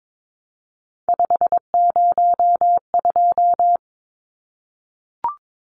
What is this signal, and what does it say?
Morse code, heard as a keyed tone, about 22 wpm, 700 Hz. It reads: 502
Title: Numbers - 3-Digit 22wpm